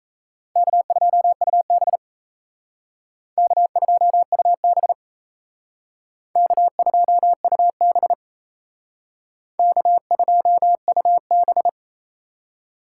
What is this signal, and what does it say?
Morse code, heard as a keyed tone, about 35 wpm, 700 Hz. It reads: K2UB K2UB K2UB K2UB
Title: Speed Racing - USA 1x2 Call Signs 28wpm